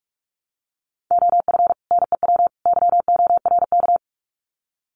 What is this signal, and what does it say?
Morse code, heard as a keyed tone, about 44 wpm, 700 Hz. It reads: OF NEW YORK
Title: Sets of 3 Words 45wpm